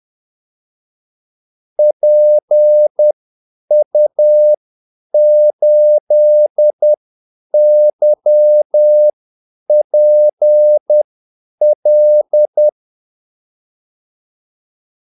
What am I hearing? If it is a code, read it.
Morse code, heard as a keyed tone, about 10 wpm, spaced as for 8 wpm, 600 Hz. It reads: PU8YPL
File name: Callsigns-10-8wpm